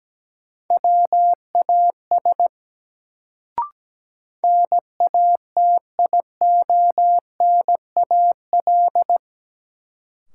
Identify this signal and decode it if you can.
Morse code, heard as a keyed tone, about 17 wpm, 700 Hz. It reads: WAS NATIONAL
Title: Sets of 2 Words - New Word Drill - 17wpm